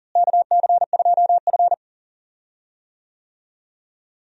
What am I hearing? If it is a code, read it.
Morse code, heard as a keyed tone, about 40 wpm, 700 Hz. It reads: KC2F
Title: USA Call Signs - 4 Characters - Repeated 2x - 40wpm